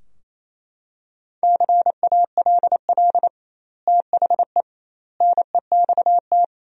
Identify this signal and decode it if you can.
Morse code, heard as a keyed tone, about 28 wpm, 700 Hz. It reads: CALL THE NEXT
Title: 2x Word-Spacing - Sentences from Top 300 Words - 28wpm